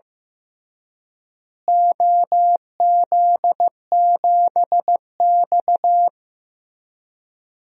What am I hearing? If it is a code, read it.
Morse code, heard as a keyed tone, about 15 wpm, 700 Hz. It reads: OZ7X
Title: DXCC-100 Call Signs - 4 Characters - Repeated 3x - 15wpm